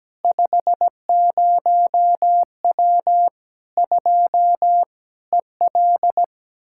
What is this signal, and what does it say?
Morse code, heard as a keyed tone, about 17 wpm, 700 Hz. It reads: 50W 2 EL
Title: Practice QSO - Exchange 2 - Info - 17wpm